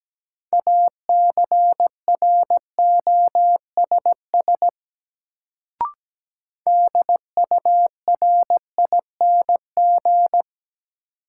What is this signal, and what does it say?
Morse code, heard as a keyed tone, about 17 wpm, 700 Hz. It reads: ACROSS DURING
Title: Sentences from Top 700 Words - New Word Drill - 17wpm